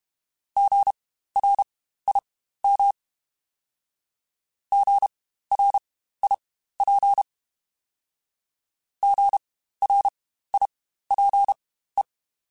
Morse code, heard as a keyed tone, about 32 wpm, spaced as for 15 wpm, 800 Hz. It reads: GRIM GRIP GRIPE
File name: Spaced Alphas / Spaced Alphas mp3s / Gs sp @30